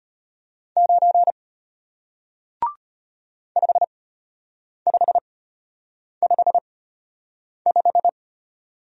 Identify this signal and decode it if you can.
Morse code, heard as a keyed tone, about 33 wpm, spaced as for 8 wpm, 700 Hz. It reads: 9 5555